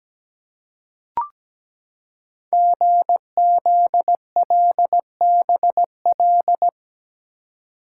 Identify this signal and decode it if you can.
Morse code, heard as a keyed tone, about 17 wpm, 700 Hz. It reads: GZLBL
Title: Code Groups - 5 Letters - 17wpm